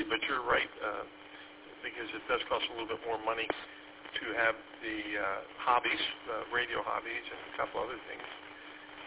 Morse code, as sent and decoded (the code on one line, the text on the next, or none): none